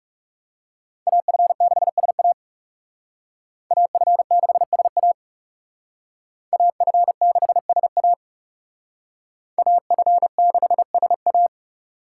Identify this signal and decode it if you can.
Morse code, heard as a keyed tone, about 35 wpm, 700 Hz. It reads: AF6SA AF6SA AF6SA AF6SA
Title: Speed Racing - USA 2x2 Call Signs 30wpm